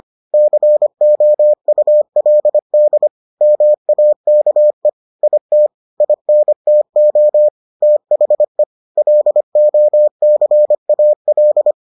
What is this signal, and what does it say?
Morse code, heard as a keyed tone, about 25 wpm, 600 Hz. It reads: COULD MAKE IT INTO THE LOCAL